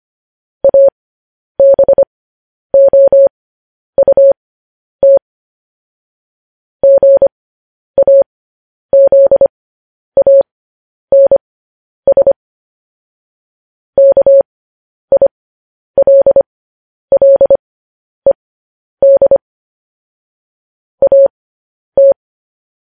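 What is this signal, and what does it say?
Morse code, heard as a keyed tone, about 25 wpm, spaced as for 10 wpm, 550 Hz. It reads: ABOUT GAZANS KILLED AT
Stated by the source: Morse code news headlines practice at 10 wpm.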